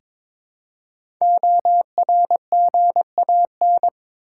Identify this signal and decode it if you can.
Morse code, heard as a keyed tone, about 22 wpm, 700 Hz. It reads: ORGAN
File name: 5 Letter Words - First 3 Letters Given - 22wpm